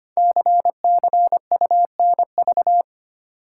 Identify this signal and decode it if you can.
Morse code, heard as a keyed tone, about 25 wpm, 700 Hz. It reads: CCUNV
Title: Code Groups - 5 Letters - 25wpm